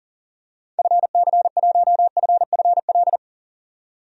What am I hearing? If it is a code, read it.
Morse code, heard as a keyed tone, about 40 wpm, 700 Hz. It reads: FC1FFL
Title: DXCC-100 Call Signs - 6 Characters - 40wpm